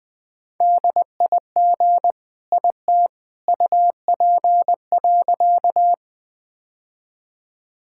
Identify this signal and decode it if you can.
Morse code, heard as a keyed tone, about 20 wpm, 700 Hz. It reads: DIG IT UP.
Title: Sentences from Top 1600 Words 20wpm